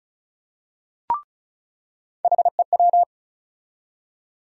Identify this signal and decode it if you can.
Morse code, heard as a keyed tone, about 34 wpm, 700 Hz. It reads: HEW
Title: USA Call Sign Postfixes - 3 Letters 35wpm